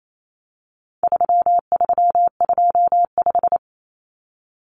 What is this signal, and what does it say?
Morse code, heard as a keyed tone, about 28 wpm, 700 Hz. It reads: 3325
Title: Common Cut Numbers - 4-Digit 28wpm